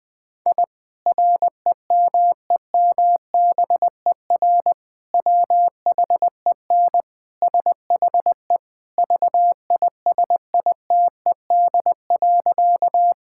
Morse code, heard as a keyed tone, about 20 wpm, 700 Hz. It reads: I REMEMBER WHEN SHE VISITED.
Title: Sentences with 2-Word Phrases - Spoken Beginning - 20wpm